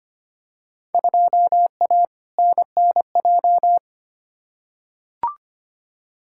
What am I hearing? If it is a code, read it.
Morse code, heard as a keyed tone, about 25 wpm, 700 Hz. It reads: 2A NNJ
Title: Field Day Exchanges - 25wpm